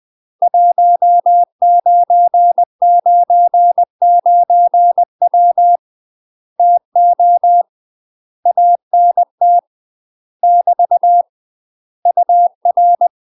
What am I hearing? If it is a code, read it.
Morse code, heard as a keyed tone, about 20 wpm, 700 Hz. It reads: 1999W TO ANT <BT> UR
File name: UK MASTER QSOs TRAINING / 20WPM / UK MASTER QSOs 0029